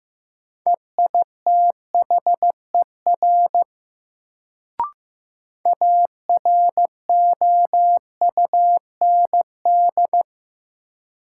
Morse code, heard as a keyed tone, about 15 wpm, 700 Hz. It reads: EITHER AROUND
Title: Sentences from Top 600 Words - New Word Drill - 15wpm